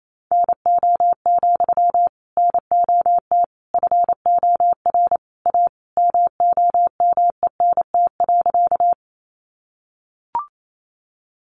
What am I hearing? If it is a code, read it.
Morse code, heard as a keyed tone, about 28 wpm, 700 Hz. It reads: NO, NOT FOR A MOMENT.
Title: Sentences from Top 1000 Words 28wpm